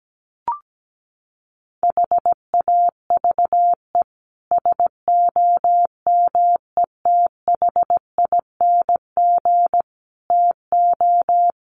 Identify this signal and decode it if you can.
Morse code, heard as a keyed tone, about 17 wpm, 690 Hz. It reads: HAVE SOMETHING TO
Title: Sets of 5 Words Encore 17wpm